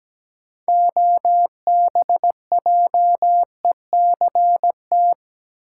Morse code, heard as a keyed tone, about 17 wpm, 700 Hz. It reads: OBJECT
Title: Sentences from Top 500 Words - New Word Drill - 17wpm